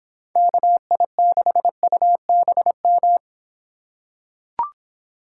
Morse code, heard as a keyed tone, about 26 wpm, 700 Hz. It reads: KI6UBM